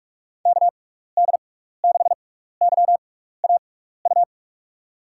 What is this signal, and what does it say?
Morse code, heard as a keyed tone, about 44 wpm, spaced as for 15 wpm, 700 Hz. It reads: KD6YAU
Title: USA Call Signs - 6x Character Spacing - 45wpm